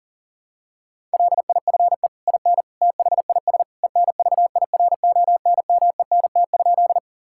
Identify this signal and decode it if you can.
Morse code, heard as a keyed tone, about 40 wpm, 700 Hz. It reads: LIFE IN THIS ENVIRONMENT?
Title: Sentences from Top 1000 Words - Review Part 5 - 40wpm